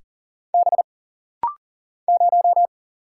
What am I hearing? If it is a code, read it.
Morse code, heard as a keyed tone, about 40 wpm, 700 Hz. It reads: B 0
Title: Single Letter-Number - Rapid-Fire 40wpm